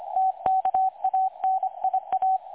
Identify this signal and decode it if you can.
Morse code, heard as a keyed tone, about 25 wpm, 760 Hz, with radio noise. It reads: TKANIA
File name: dlazaawansowanych / nagraniazpasma / websdr recording 3565.0kHz